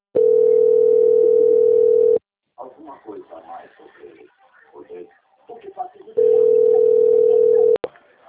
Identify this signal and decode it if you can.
Ringback tone (North America), 440 + 480 Hz, with speech.
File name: out- - - - - .57479